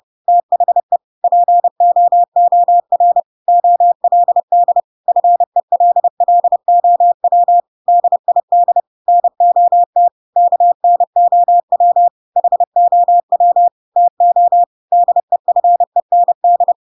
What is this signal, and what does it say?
Morse code, heard as a keyed tone, about 30 wpm, 700 Hz. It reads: THE POOR OLD FELLOW DID NOT KNOW HOW TO DEFEND
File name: Pinocchio / Pinocchio ch3 30 30 ik1hge